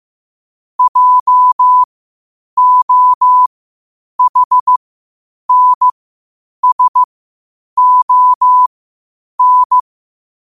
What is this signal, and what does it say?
Morse code, heard as a keyed tone, about 15 wpm, spaced as for 8 wpm, 1000 Hz. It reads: JOHNSON
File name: qotdfiles / 8wpm